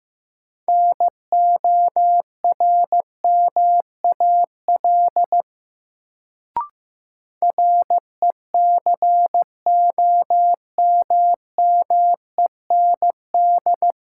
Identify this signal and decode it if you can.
Morse code, heard as a keyed tone, about 15 wpm, 700 Hz. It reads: NORMAL RECOMMEND
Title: Sentences from Top 1200 Words - New Word Drill - 15wpm